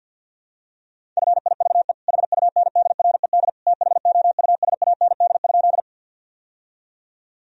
Morse code, heard as a keyed tone, about 50 wpm, 700 Hz. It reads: FIVE HUNDRED THOUSAND?
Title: Sentences from Top 1000 Words - Review Part 3 - 50wpm